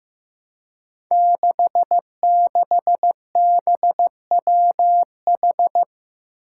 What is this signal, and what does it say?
Morse code, heard as a keyed tone, about 15 wpm, 700 Hz. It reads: 66BWH
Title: Five Challenging Letter-Number Combinations 15wpm